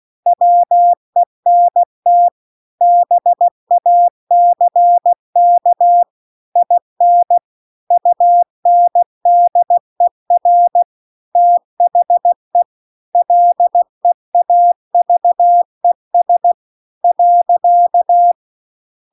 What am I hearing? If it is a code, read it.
Morse code, heard as a keyed tone, about 16 wpm, 700 Hz. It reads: WENT BACK IN UNDER THE LEAVES .